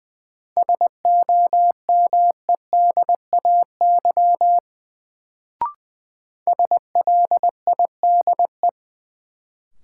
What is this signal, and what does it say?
Morse code, heard as a keyed tone, about 20 wpm, 700 Hz. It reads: SOMEDAY SLIDE
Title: Sentences from Top 1600 Words - New Word Drill - 20wpm